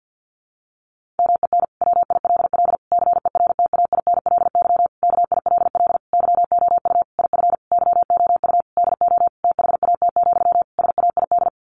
Morse code, heard as a keyed tone, about 50 wpm, 700 Hz. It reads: MEN WILL CERTAINLY KILL YOU IF YOU DO THAT, SAID
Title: Sentences from Top 1500 Words 50wpm